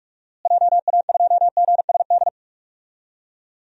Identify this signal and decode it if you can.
Morse code, heard as a keyed tone, about 46 wpm, 700 Hz. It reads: JA2GSD
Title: DXCC-100 Call Signs - 6 Characters - Repeated 2x - 45wpm